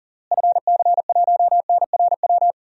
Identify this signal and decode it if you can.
Morse code, heard as a keyed tone, about 40 wpm, 700 Hz. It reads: FC1NRW
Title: DXCC-100 Call Signs - 6 Characters - Repeated 3x - 40wpm